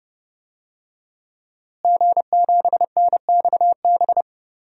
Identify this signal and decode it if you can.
Morse code, heard as a keyed tone, about 31 wpm, 700 Hz. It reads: G7NXB
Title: DXCC-100 Call Signs - 5 Characters - Repeated 2x - 30wpm